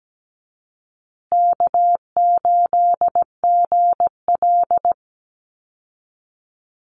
Morse code, heard as a keyed tone, about 17 wpm, 700 Hz. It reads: K8GL
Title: Easy Call Signs Repeated 2x - 17wpm